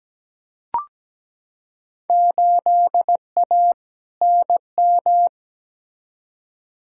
Morse code, heard as a keyed tone, about 17 wpm, 700 Hz. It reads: 8A NM
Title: Field Day Exchanges Encore - 17wpm